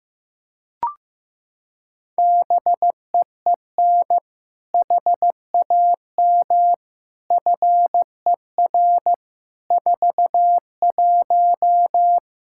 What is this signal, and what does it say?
Morse code, heard as a keyed tone, about 15 wpm, 700 Hz. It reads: BEEN HAM FER 41
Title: Practice QSO - Exchange 3 - More Info - 15wpm